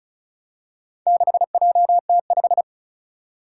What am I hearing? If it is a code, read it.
Morse code, heard as a keyed tone, about 36 wpm, 700 Hz. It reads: 6JT5
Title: Four Challenging Letter-Number Combinations 35wpm